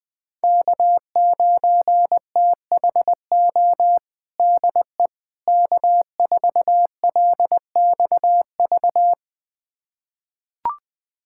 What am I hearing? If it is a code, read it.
Morse code, heard as a keyed tone, about 20 wpm, 700 Hz. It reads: K9THO DE K4LXV
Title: USA Callsign DE Callsign 20wpm